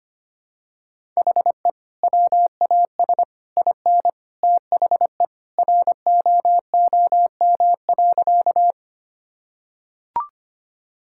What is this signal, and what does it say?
Morse code, heard as a keyed tone, about 25 wpm, 700 Hz. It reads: HE WAS IN THE ROOM.
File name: practice / 25wpm / Sentences with 2-Word Phrases - 25wpm